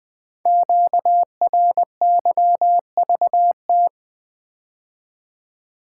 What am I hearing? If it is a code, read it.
Morse code, heard as a keyed tone, about 20 wpm, 700 Hz. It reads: QRYVT
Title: Code Groups - 5 Letter:Numbers - 20wpm